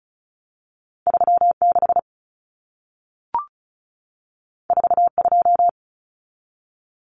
Morse code, heard as a keyed tone, about 36 wpm, 700 Hz. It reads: 36 42
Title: Common Cut Numbers - 2-Digit 35wpm